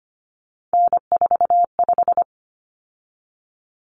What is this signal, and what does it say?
Morse code, heard as a keyed tone, about 25 wpm, 700 Hz. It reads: N45
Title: Common Cut Numbers - 3-Digit 25wpm